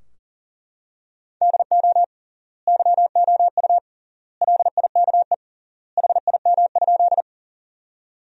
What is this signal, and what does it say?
Morse code, heard as a keyed tone, about 40 wpm, 700 Hz. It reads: DO YOU LIKE HIM?
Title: 3x Word-Spacing - Sentences from Top 100 Words - 40wpm